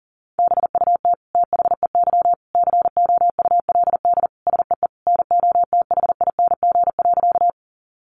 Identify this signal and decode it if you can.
Morse code, heard as a keyed tone, about 40 wpm, 700 Hz. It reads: BUT THEY COULD SEE NOTHING.
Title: Sentences from Top 500 Words 40wpm